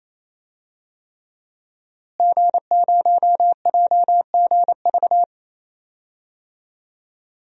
Morse code, heard as a keyed tone, about 28 wpm, 700 Hz. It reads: G0JGV
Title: DXCC-100 Call Signs - 5 Characters - Repeated 3x - 28wpm